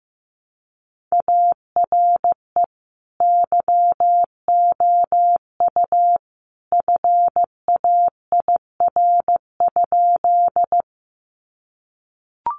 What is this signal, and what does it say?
Morse code, heard as a keyed tone, about 15 wpm, 700 Hz. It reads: ARE YOU FAIR?
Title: Sentences from Top 1300 Words 15wpm